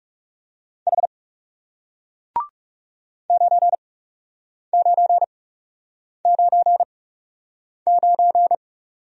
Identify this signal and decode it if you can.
Morse code, heard as a keyed tone, about 41 wpm, spaced as for 8 wpm, 700 Hz. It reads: H 9999